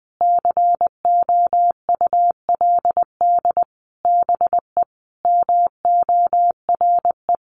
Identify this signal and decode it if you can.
Morse code, heard as a keyed tone, about 20 wpm, 700 Hz. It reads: COULD BE MORE
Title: Sentences from Top 1200 Words 20wpm